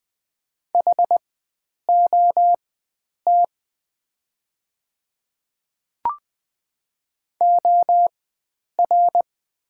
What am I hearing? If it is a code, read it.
Morse code, heard as a keyed tone, about 20 wpm, spaced as for 9 wpm, 690 Hz. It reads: HOT OR
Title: 4x Character Spacing - Top 100 Words - 20wpm